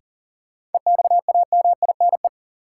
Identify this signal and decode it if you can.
Morse code, heard as a keyed tone, about 40 wpm, 700 Hz. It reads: EXAMINE
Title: Sentences from Top 1600 Words - New Word Drill Alt - 40wpm